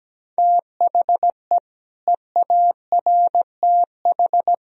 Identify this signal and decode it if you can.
Morse code, heard as a keyed tone, about 17 wpm, 700 Hz. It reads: THE EARTH
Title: Sentences from Top 1000 Words - Review Part 3 - 17wpm